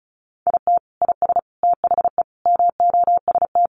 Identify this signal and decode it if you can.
Morse code, heard as a keyed tone, about 34 wpm, 700 Hz. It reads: IT IS THE MOST
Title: Sentences from Top 1500 Words 35wpm